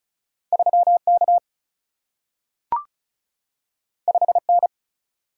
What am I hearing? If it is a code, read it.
Morse code, heard as a keyed tone, about 36 wpm, 700 Hz. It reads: 3 K 5 N
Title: Single-Letter Single-Number 35wpm